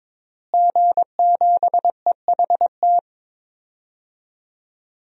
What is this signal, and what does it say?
Morse code, heard as a keyed tone, about 22 wpm, 700 Hz. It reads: G7EHT
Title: DXCC-100 Call Signs - 5 Characters - Repeated 2x - 22wpm